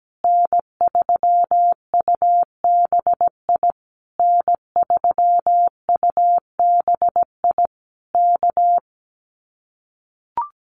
Morse code, heard as a keyed tone, about 17 wpm, 700 Hz. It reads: N3UBI N3UBI K
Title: Common QSO Phrases Encore 17wpm